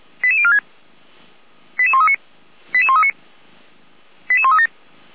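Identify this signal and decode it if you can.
Five-tone selective call (ZVEI), heard as quick runs of five tones, tones of about 70 ms. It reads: 88046 88139 88139 88137
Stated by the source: ZVEI five-tone call received by SDR, from audio.f85360.feuerwehr-orlamuende.org.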